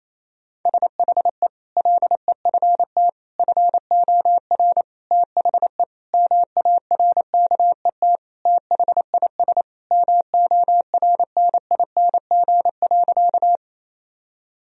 Morse code, heard as a keyed tone, about 28 wpm, 700 Hz. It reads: SHE LEFT FOR THE MARKET THIS MORNING.
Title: Sentences from Top 1000 Words - Review Part 3 - 28wpm